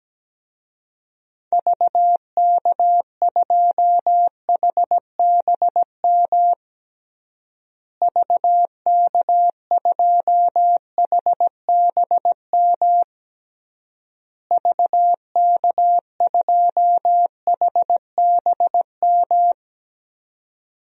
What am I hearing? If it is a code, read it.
Morse code, heard as a keyed tone, about 17 wpm, 700 Hz. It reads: VK2HBM VK2HBM VK2HBM
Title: DXCC-100 Call Signs - 6 Characters - Repeated 3x - 17wpm